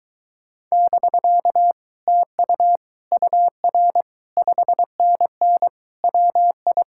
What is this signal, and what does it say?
Morse code, heard as a keyed tone, about 23 wpm, 700 Hz. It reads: <BK> TU UR 5NN WI